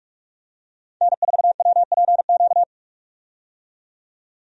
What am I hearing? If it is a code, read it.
Morse code, heard as a keyed tone, about 41 wpm, 700 Hz. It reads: N4WPQ
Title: USA Call Signs - 5 Characters - 45wpm